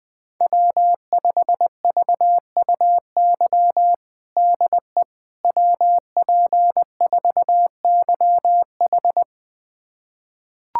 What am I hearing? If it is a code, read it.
Morse code, heard as a keyed tone, about 20 wpm, 700 Hz. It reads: W5VUY DE WP4YH
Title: USA Callsign DE Callsign - 5 Characters - 20wpm